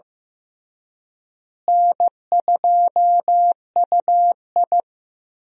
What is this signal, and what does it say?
Morse code, heard as a keyed tone, about 16 wpm, 700 Hz. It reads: N2UI
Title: USA Call Signs - 4 Characters - Repeated 3x - 15wpm